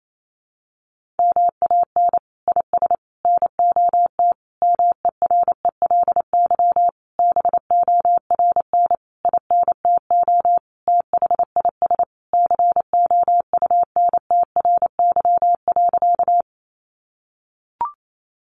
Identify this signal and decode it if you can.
Morse code, heard as a keyed tone, about 28 wpm, 700 Hz. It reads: MAN IS NOT MERELY BORN INTO THIS COUNTRY.
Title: Sentences from Top 1400 Words 28wpm